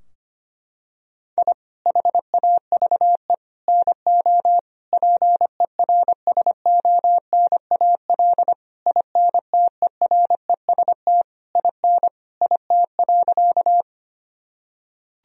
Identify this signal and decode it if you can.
Morse code, heard as a keyed tone, about 25 wpm, 700 Hz. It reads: I HAVE NO PERSONAL INTEREST IN IT.
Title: Sentences from Top 1000 Words - Review Part 4 - 25wpm